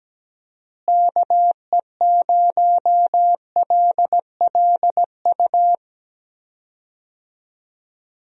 Morse code, heard as a keyed tone, about 17 wpm, 700 Hz. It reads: KE0LLU
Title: USA Call Signs - 6 Characters - Repeated 3x - 17wpm